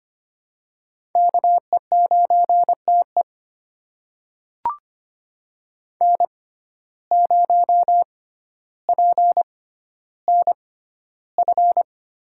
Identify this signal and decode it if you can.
Morse code, heard as a keyed tone, about 25 wpm, spaced as for 13 wpm, 700 Hz. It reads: KE9TE N0PNF
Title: USA Call Signs - 5 Characters - 6x Character Spacing - 25wpm